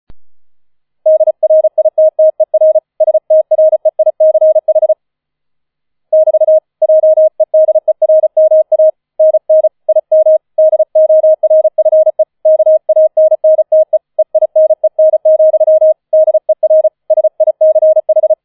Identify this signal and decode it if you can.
Morse code, heard as a keyed tone, about 34 wpm, 630 Hz. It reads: DRITTER STREICH <BT> JEDERMANN IM DORFE KANNTE EINEN, DER SICH